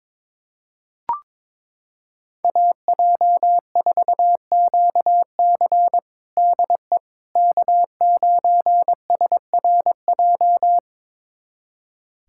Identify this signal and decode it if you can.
Morse code, heard as a keyed tone, about 22 wpm, 700 Hz. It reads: AJ4QC DE K9SRJ
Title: USA Callsign DE Callsign - 5 Characters - 22wpm